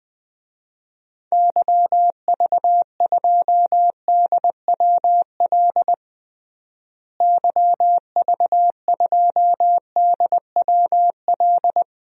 Morse code, heard as a keyed tone, about 21 wpm, 700 Hz. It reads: YV2DWL YV2DWL
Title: DXCC-100 Call Signs - 6 Characters - Repeated 3x - 20wpm